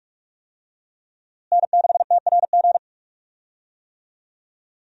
Morse code, heard as a keyed tone, about 45 wpm, 700 Hz. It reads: N6TRG
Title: USA Call Signs - 5 Characters - Repeated 3x - 45wpm